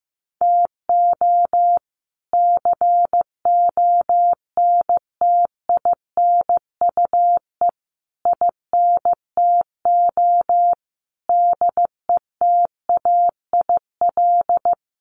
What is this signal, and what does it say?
Morse code, heard as a keyed tone, about 14 wpm, 700 Hz. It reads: TO CONTINUE INTO DETAIL
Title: Sentences from Top 1500 Words 15wpm